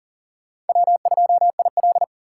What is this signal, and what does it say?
Morse code, heard as a keyed tone, about 50 wpm, 700 Hz. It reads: W2IL
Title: Speed Racing - USA 1x2 Call Signs 40wpm